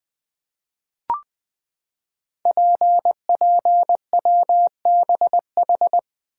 Morse code, heard as a keyed tone, about 20 wpm, 700 Hz. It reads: PPWBH
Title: Code Groups - 5 Letters - 20wpm